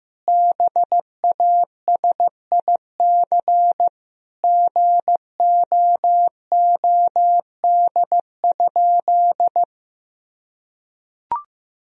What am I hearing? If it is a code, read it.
Morse code, heard as a keyed tone, about 15 wpm, 700 Hz. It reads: BASIC GOOD?
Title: Sentences from Top 1000 Words - Review Part 3 - 15wpm